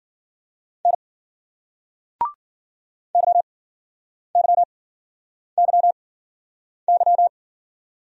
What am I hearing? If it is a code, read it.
Morse code, heard as a keyed tone, about 56 wpm, spaced as for 9 wpm, 700 Hz. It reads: N YYYY